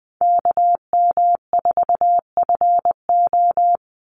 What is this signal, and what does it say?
Morse code, heard as a keyed tone, about 20 wpm, 700 Hz. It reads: KM4FO
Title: Call Signs Repeated 2x - 20wpm